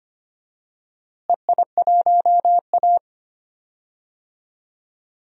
Morse code, heard as a keyed tone, about 25 wpm, 700 Hz. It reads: EI1A
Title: DXCC-100 Call Signs - 4 Characters - Repeated 3x - 25wpm